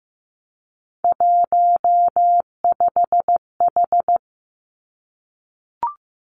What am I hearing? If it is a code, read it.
Morse code, heard as a keyed tone, about 15 wpm, 700 Hz. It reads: 15H
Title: Three Challenging Letter-Number Combinations 15wpm